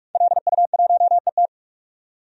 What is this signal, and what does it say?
Morse code, heard as a keyed tone, about 46 wpm, 700 Hz. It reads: LU1ET
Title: DXCC-100 Call Signs - 5 Characters - 45wpm